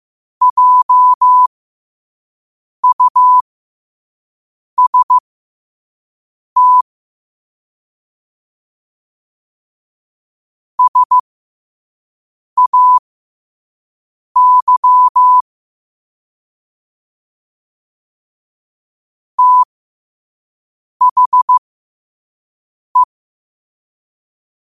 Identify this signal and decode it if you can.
Morse code, heard as a keyed tone, about 15 wpm, spaced as for 5 wpm, 1000 Hz. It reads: JUST SAY THE
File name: qotdfiles / 5wpm